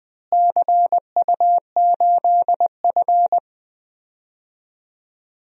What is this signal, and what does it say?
Morse code, heard as a keyed tone, about 20 wpm, 700 Hz. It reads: CU8F
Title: DXCC-100 Call Signs - 4 Characters - Repeated 2x - 20wpm